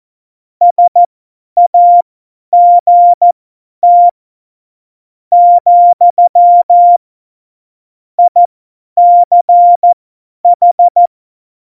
Morse code, heard as a keyed tone, about 14 wpm, spaced as for 10 wpm, 700 Hz. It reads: SAGT , ICH
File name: Morsekurs / Witze / Witz 22. Januar 2019, 50 F70